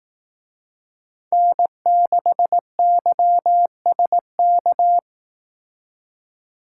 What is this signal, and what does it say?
Morse code, heard as a keyed tone, about 18 wpm, 700 Hz. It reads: N6YSK
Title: Speed Racing - USA 1x3 Call Signs 15wpm